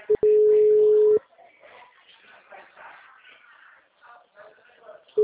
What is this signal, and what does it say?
Ringback tone (Europe, 425 Hz), with speech.